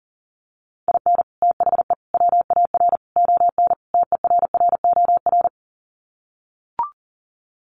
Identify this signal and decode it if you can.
Morse code, heard as a keyed tone, about 37 wpm, 700 Hz. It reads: IN THE WAR ON TERROR
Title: Sets of 5 Words Encore2 40wpm